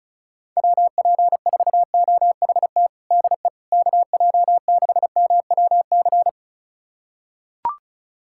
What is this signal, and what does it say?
Morse code, heard as a keyed tone, about 34 wpm, 700 Hz. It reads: WP4OHT DE KJ6MWC
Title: USA Callsign DE Callsign 35wpm